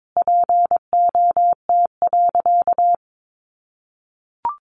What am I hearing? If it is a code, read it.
Morse code, heard as a keyed tone, about 22 wpm, 700 Hz. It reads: POT.
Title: Sentences from Top 1000 Words 22wpm